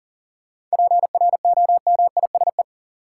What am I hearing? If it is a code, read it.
Morse code, heard as a keyed tone, about 40 wpm, 700 Hz. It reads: PROMISE
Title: Seven Letter Words - 40wpm